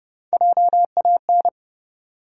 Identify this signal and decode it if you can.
Morse code, heard as a keyed tone, about 30 wpm, 700 Hz. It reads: JAN
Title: USA Common Names - Max 3 Letters - 30wpm